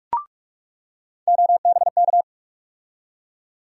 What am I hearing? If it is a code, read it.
Morse code, heard as a keyed tone, about 47 wpm, 700 Hz. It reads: OBK
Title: USA Call Sign Postfixes - 3 Letters 45wpm